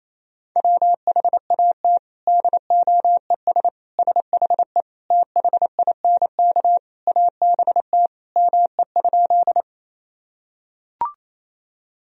Morse code, heard as a keyed tone, about 28 wpm, 700 Hz. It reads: WHAT DOES SHE THINK ABT ME?
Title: Sentences from Top 200 Words and Abbreviations 28wpm